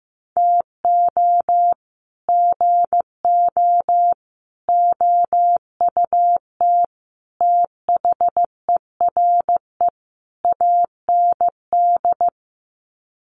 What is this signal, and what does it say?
Morse code, heard as a keyed tone, about 15 wpm, 700 Hz. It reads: TO GO OUT THERE AND
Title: Sets of 5 Words Encore2 15wpm